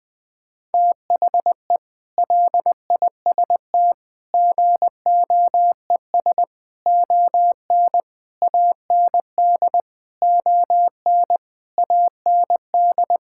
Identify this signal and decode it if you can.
Morse code, heard as a keyed tone, about 20 wpm, 700 Hz. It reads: THE LIST GOES ON AND ON AND
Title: Sentences with 8-Word Phrases - 20wpm